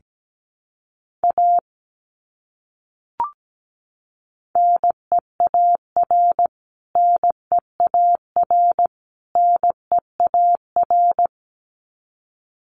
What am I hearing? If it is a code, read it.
Morse code, heard as a keyed tone, about 17 wpm, 700 Hz. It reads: A NEAR NEAR NEAR
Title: Top 200 Words Repeated 17wpm